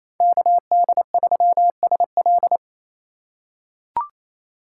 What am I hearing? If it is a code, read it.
Morse code, heard as a keyed tone, about 28 wpm, 700 Hz. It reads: KD3SL
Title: USA Call Signs - 3x Character Spacing - 28wpm